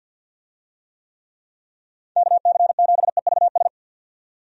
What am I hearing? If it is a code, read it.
Morse code, heard as a keyed tone, about 51 wpm, 700 Hz. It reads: KC7EVS
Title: USA Call Signs - 6 Characters - 50wpm